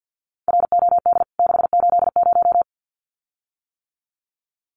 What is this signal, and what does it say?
Morse code, heard as a keyed tone, about 49 wpm, 700 Hz. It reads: ROD 680
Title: CWT Exchange 50wpm